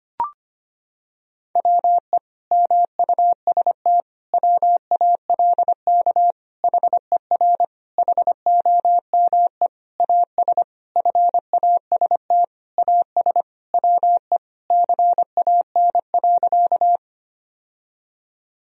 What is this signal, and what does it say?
Morse code, heard as a keyed tone, about 25 wpm, 700 Hz. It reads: WE MUST WALK HER HOME AS FAST AS WE CAN.
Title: Sentences from Top 1000 Words - Review Part 2 - 25wpm